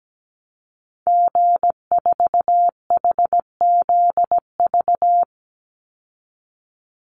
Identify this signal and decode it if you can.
Morse code, heard as a keyed tone, about 17 wpm, 700 Hz. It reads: G4HZV
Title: Call Signs Repeated 3x - 17wpm